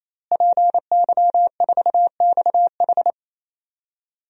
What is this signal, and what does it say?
Morse code, heard as a keyed tone, about 28 wpm, 700 Hz. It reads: PY4XH
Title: DXCC-100 Call Signs - 5 Characters - Repeated 3x - 28wpm